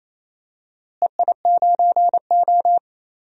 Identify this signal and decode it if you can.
Morse code, heard as a keyed tone, about 28 wpm, 700 Hz. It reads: EI9O
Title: DXCC-100 Call Signs - 4 Characters - Repeated 2x - 28wpm